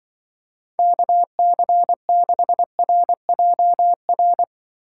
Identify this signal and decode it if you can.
Morse code, heard as a keyed tone, about 24 wpm, 700 Hz. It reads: KC6RJR
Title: Speed Racing - USA 2x3 Call Signs 20wpm